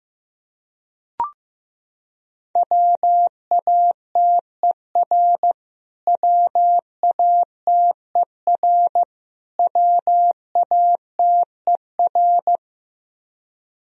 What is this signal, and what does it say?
Morse code, heard as a keyed tone, about 15 wpm, 700 Hz. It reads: WATER WATER WATER
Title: Top 100 Words Repeated 15wpm